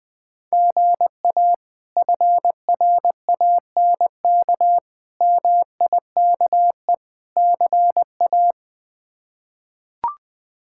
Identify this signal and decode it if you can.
Morse code, heard as a keyed tone, about 20 wpm, 700 Hz. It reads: GA FRANK MIKE CA
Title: K1USN SST QSO Practice - 20wpm